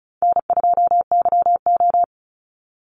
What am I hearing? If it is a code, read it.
Morse code, heard as a keyed tone, about 34 wpm, 700 Hz. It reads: N2YO
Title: Easy Call Signs Repeated 2x - 35wpm